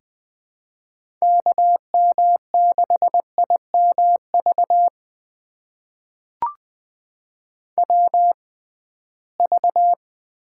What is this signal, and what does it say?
Morse code, heard as a keyed tone, about 21 wpm, spaced as for 15 wpm, 700 Hz. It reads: KM6IMV WV
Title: USA Call Signs - 6 Characters - 6x Character Spacing - 20wpm